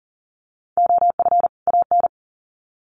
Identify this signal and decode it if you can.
Morse code, heard as a keyed tone, about 40 wpm, 700 Hz. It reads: OF AN
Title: Sets of 2 Words 40wpm